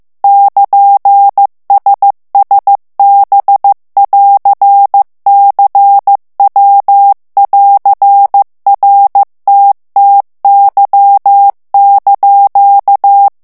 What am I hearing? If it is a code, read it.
Morse code, heard as a keyed tone, about 15 wpm, 800 Hz. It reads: <KN>SSB<AR>CW<AR>RTTY)
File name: morse mp3 / ITALIEN texte clair 2 vitesse 14 mots minute